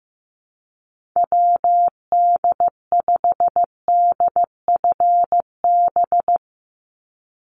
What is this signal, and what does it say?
Morse code, heard as a keyed tone, about 15 wpm, 700 Hz. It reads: WD5DFB
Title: USA Call Signs 15wpm